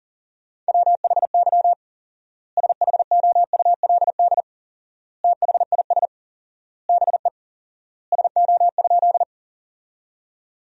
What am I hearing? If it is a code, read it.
Morse code, heard as a keyed tone, about 40 wpm, 700 Hz. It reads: WHY SHOULD THIS BE SO?
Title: 4x Word-Spacing - Sentences from Top 300 Words - 40wpm